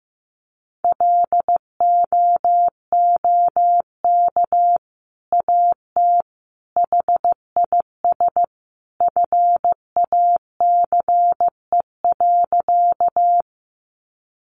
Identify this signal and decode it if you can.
Morse code, heard as a keyed tone, about 15 wpm, 700 Hz. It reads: LOOK AT HIS FACE.
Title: Sentences from Top 300 Words 15wpm